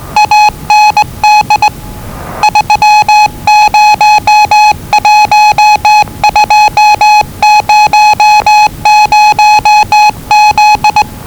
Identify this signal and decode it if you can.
Morse code, heard as a keyed tone, about 20 wpm, 840 Hz. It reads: AND 301200Z
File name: WellingtonRadioZlwCloseDownOnHfAndMf / 01-Zlw 4277khz 417.5 qrt until 0925z